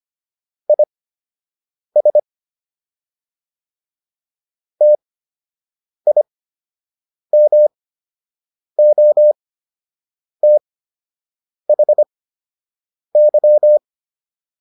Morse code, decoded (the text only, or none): IS TIMOTHY